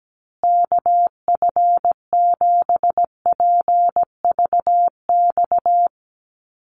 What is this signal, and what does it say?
Morse code, heard as a keyed tone, about 17 wpm, 700 Hz. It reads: KF7PVX
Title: USA Call Signs Repeated 3x 17wpm